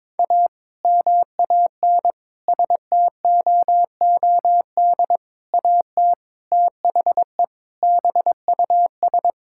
Morse code, heard as a keyed tone, about 21 wpm, 700 Hz. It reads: A MAN STOOD AT THE BUS
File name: practice / 22wpm / Sentences with 2-Word Phrases - 22wpm